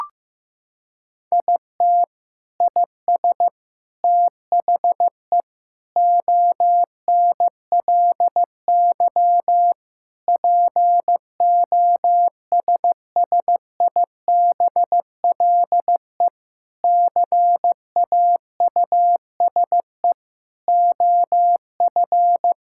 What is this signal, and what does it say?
Morse code, heard as a keyed tone, about 15 wpm, 700 Hz. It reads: IT IS THE ONLY POSSIBLE CAUSE OF
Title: Sentences from Top 1600 Words 15wpm